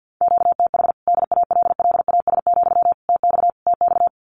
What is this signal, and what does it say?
Morse code, heard as a keyed tone, about 50 wpm, 700 Hz. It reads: QTH DALLAS, TX TX
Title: Common QSO Phrases Encore 50wpm